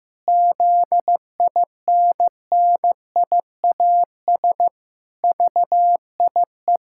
Morse code, heard as a keyed tone, about 15 wpm, 700 Hz. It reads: ZINNIAS VIE
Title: Pangram Challenge v2 - 15wpm